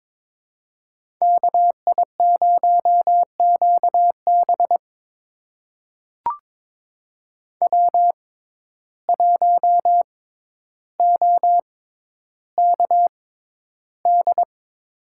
Morse code, heard as a keyed tone, about 22 wpm, spaced as for 11 wpm, 700 Hz. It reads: KI0QB W1OKD
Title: USA Call Signs - 5 Characters - 6x Character Spacing - 22wpm